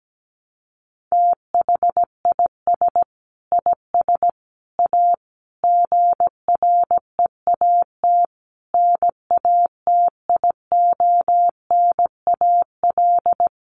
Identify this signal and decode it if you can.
Morse code, heard as a keyed tone, about 17 wpm, 700 Hz. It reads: THIS IS A GREAT NATIONAL
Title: Sentences from Top 1300 Words 17wpm